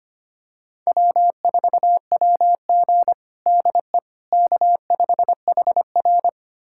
Morse code, heard as a keyed tone, about 25 wpm, 700 Hz. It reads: W4WG DE K5HR
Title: USA Callsign DE Callsign - 4 Characters - 25wpm